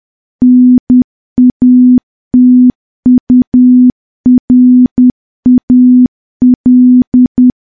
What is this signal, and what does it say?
Morse code, heard as a keyed tone, about 10 wpm, 250 Hz. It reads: NATURAL
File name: morse / morse-training-10WPM